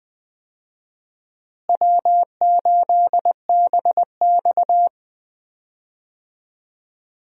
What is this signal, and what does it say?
Morse code, heard as a keyed tone, about 20 wpm, 700 Hz. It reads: W8BX
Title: USA Call Signs - 4 Characters - Repeated 3x - 20wpm